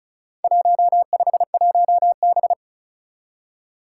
Morse code, heard as a keyed tone, about 34 wpm, 700 Hz. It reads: 151B
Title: Four Challenging Letter-Number Combinations 35wpm